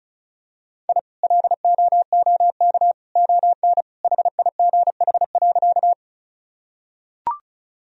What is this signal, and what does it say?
Morse code, heard as a keyed tone, about 36 wpm, 700 Hz. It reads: I LOOK ON HIGH.
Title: Sentences from Top 1000 Words - Review Part 1 - 35wpm